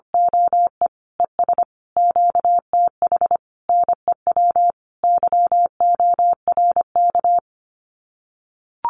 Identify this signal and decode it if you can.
Morse code, heard as a keyed tone, about 25 wpm, 700 Hz. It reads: OE ES QTH NEW YORK
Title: Common QSO Phrases Encore 25wpm